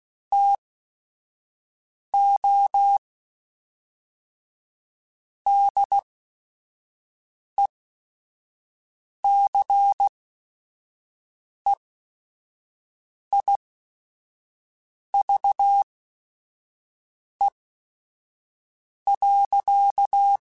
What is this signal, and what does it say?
Morse code, heard as a keyed tone, about 16 wpm, spaced as for 5 wpm, 800 Hz. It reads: TO DECEIVE.
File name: Quotes / Quotes-mp3s / Deceive @5